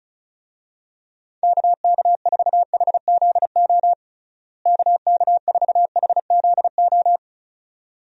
Morse code, heard as a keyed tone, about 34 wpm, 700 Hz. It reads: KK4HZO KK4HZO
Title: USA Call Signs - 6 Characters - Repeated 2x - 35wpm